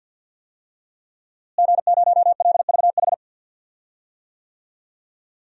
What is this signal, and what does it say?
Morse code, heard as a keyed tone, about 48 wpm, 700 Hz. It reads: G0LVH
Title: DXCC-100 Call Signs - 5 Characters - Repeated 2x - 50wpm